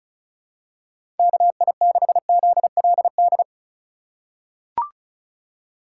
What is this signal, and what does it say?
Morse code, heard as a keyed tone, about 36 wpm, 700 Hz. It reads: KI6ZLD
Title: USA Call Signs - 6 Characters - 3x Character Spacing - 35wpm